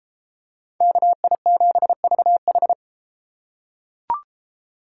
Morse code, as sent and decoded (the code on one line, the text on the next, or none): -.- .. --... ...- ....
KI7VH